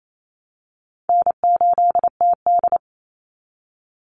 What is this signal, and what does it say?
Morse code, heard as a keyed tone, about 28 wpm, 690 Hz. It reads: N8TD
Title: Call Signs Repeated 3x - 28wpm